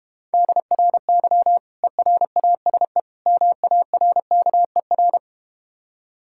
Morse code, heard as a keyed tone, about 31 wpm, 700 Hz. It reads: DRY ERASE MARKER